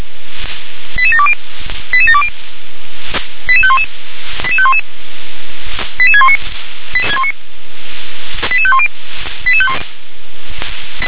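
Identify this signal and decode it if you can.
Five-tone selective call (ZVEI), heard as quick runs of five tones, tones of about 80 ms. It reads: 88720 88720 88511 88511 88619 88510
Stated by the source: ZVEI five-tone selective call received on SDR.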